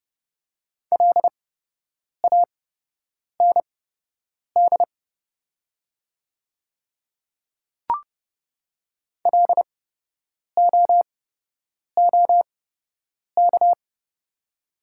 Morse code, heard as a keyed tone, about 29 wpm, spaced as for 8 wpm, 700 Hz. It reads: LAND LOOK
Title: 8x Character Spacing - Top 200 Words - 30wpm